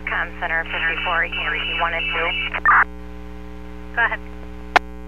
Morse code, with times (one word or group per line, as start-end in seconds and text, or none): none